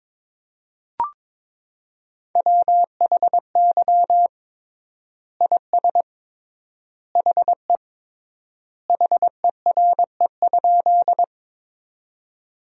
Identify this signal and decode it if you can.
Morse code, heard as a keyed tone, about 22 wpm, 700 Hz. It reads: WHY IS HE HERE?
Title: 3x Word-Spacing - Sentences from Top 200 Words - 22wpm